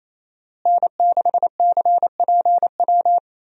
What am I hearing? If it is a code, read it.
Morse code, heard as a keyed tone, about 28 wpm, 700 Hz. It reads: N6CPW
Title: USA Call Signs - 5 Characters - Repeated 2x - 28wpm